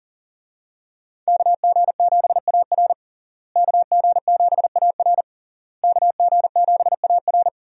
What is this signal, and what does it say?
Morse code, heard as a keyed tone, about 40 wpm, 700 Hz. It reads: KG7AR KG7AR KG7AR
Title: USA Call Signs - 5 Characters - Repeated 3x - 40wpm